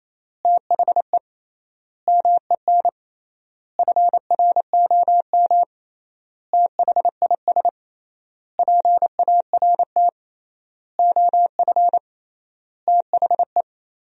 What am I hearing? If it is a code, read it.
Morse code, heard as a keyed tone, about 28 wpm, 700 Hz. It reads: THE MEN FROM THIS PART OF THE
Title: 3x Word-Spacing - Sentences from Top 300 Words - 28wpm